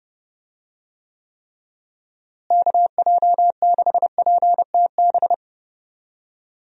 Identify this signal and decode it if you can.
Morse code, heard as a keyed tone, about 30 wpm, 700 Hz. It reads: KJ6PTB
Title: USA Call Signs - 6 Characters - Repeated 3x - 30wpm